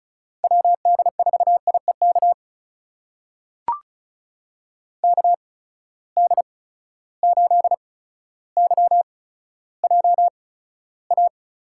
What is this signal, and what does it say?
Morse code, heard as a keyed tone, about 36 wpm, spaced as for 15 wpm, 700 Hz. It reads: WD4IEK KD8YJA
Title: USA Call Signs - 6 Characters - 8x Character Spacing - 35wpm